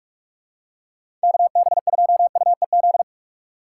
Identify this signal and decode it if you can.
Morse code, heard as a keyed tone, about 46 wpm, 700 Hz. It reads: KB2UEZ
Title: USA Call Signs - 6 Characters - 2x Character Spacing - 45wpm